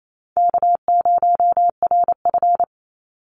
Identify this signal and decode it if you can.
Morse code, heard as a keyed tone, about 28 wpm, 700 Hz. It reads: K0RF
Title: Call Signs - 28wpm